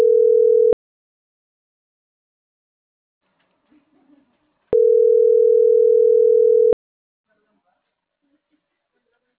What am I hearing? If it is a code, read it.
Ringback tone (North America), 440 + 480 Hz.